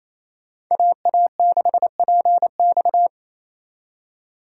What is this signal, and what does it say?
Morse code, heard as a keyed tone, about 33 wpm, 700 Hz. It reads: AA6PX
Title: Speed Racing - USA Call Signs 28wpm